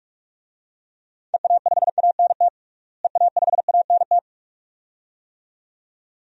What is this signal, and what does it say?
Morse code, heard as a keyed tone, about 41 wpm, 700 Hz. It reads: EA5ANT EA5ANT
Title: DXCC-100 Call Signs - 6 Characters - Repeated 2x - 45wpm